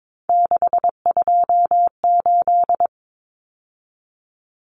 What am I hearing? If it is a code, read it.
Morse code, heard as a keyed tone, about 22 wpm, 700 Hz. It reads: 628
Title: Numbers - 3-Digit 22wpm